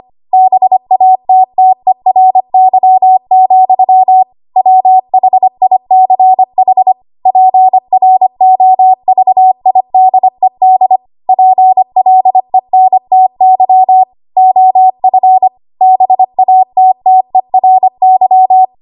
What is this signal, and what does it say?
Morse code, heard as a keyed tone, about 25 wpm, 750 Hz. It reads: BATTERY, WHICH PROVIDED PLENTY OF BATTERY
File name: Morse / 25WPM